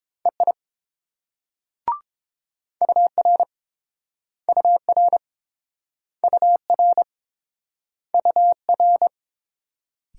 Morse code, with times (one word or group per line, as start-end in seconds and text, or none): 0.3-0.5 EI
2.8-3.4 UR
4.5-5.2 UR
6.2-7.0 UR
8.1-9.1 UR